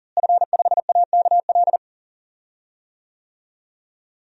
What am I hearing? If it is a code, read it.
Morse code, heard as a keyed tone, about 40 wpm, 700 Hz. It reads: F5AKL
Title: DXCC-100 Call Signs - 5 Characters - Repeated 2x - 40wpm